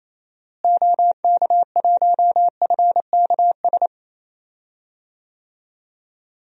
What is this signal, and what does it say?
Morse code, heard as a keyed tone, about 28 wpm, 700 Hz. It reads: OK1FKS
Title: DXCC-100 Call Signs - 6 Characters - Repeated 2x - 28wpm